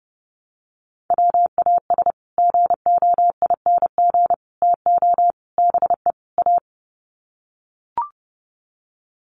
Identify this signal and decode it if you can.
Morse code, heard as a keyed tone, about 30 wpm, 700 Hz. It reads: WAS GOING TO BE A
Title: Sets of 5 Words 30wpm